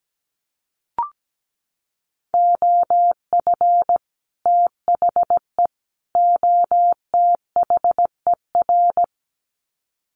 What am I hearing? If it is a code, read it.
Morse code, heard as a keyed tone, about 17 wpm, 700 Hz. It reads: OF THE OTHER
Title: Sets of 3 Words 17wpm